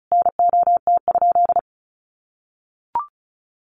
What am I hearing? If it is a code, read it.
Morse code, heard as a keyed tone, about 34 wpm, 700 Hz. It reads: NOT?
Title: Sentences from Top 1500 Words 35wpm